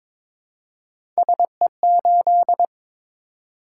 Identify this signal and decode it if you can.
Morse code, heard as a keyed tone, about 22 wpm, 700 Hz. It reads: SE8
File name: Three Letter-Number Combinations - 22wpm